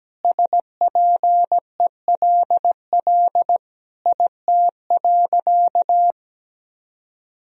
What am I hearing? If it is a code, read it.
Morse code, heard as a keyed tone, about 17 wpm, 700 Hz. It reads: SPELL IT.
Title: Sentences from Top 1000 Words - Review Part 1 - 17wpm